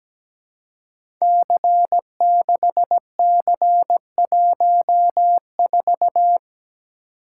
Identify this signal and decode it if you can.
Morse code, heard as a keyed tone, about 17 wpm, 700 Hz. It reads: C6C14
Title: Code Groups - 5 Letter:Numbers - 17wpm